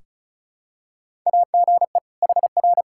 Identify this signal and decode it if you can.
Morse code, heard as a keyed tone, about 36 wpm, 700 Hz. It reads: AGE HR
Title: Practice QSO - Exchange 3 - More Info - 35wpm